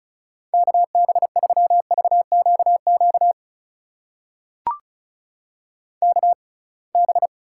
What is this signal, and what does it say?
Morse code, heard as a keyed tone, about 34 wpm, spaced as for 26 wpm, 700 Hz. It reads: KB3VQQ KB
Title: USA Call Signs - 6x Character Spacing - 35wpm